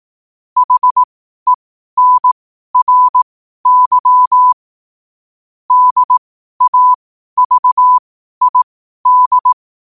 Morse code, heard as a keyed tone, about 18 wpm, spaced as for 12 wpm, 1000 Hz. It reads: HENRY DAVID
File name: qotdfiles / 12wpm